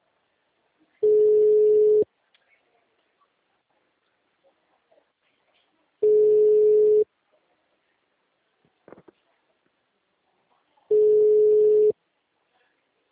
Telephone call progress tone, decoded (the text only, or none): ringback tone